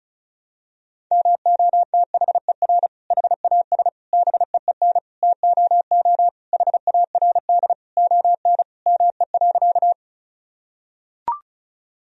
Morse code, heard as a keyed tone, about 36 wpm, 700 Hz. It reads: MOTHER HAS BEEN TOO HARD ON ME.
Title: Sentences from Top 1000 Words - Review Part 1 - 35wpm